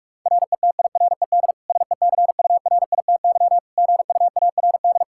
Morse code, heard as a keyed tone, about 46 wpm, 700 Hz. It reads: RETIRED SECURITY GUARD
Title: K1USN SST QSO Practice 45wpm